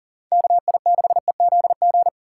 Morse code, heard as a keyed tone, about 40 wpm, 700 Hz. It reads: KI6EZG
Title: USA Call Signs - 6 Characters - 3x Character Spacing - 40wpm